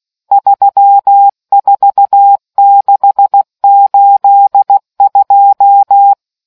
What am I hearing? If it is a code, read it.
Morse code, heard as a keyed tone, about 16 wpm, 800 Hz. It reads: 34682